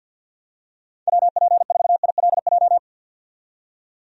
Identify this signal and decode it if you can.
Morse code, heard as a keyed tone, about 53 wpm, 700 Hz. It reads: WP4ILJ
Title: USA Call Signs - 6 Characters - 50wpm